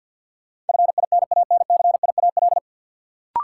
Morse code, heard as a keyed tone, about 51 wpm, 700 Hz. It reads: FINANCIAL
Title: Sentences from Top 1000 Words - Review Part 5 - New Word Drill - 50wpm